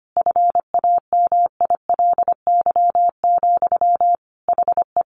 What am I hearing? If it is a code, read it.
Morse code, heard as a keyed tone, about 25 wpm, 700 Hz. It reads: FAMILY, HE
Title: Sentences from Top 1100 Words 25wpm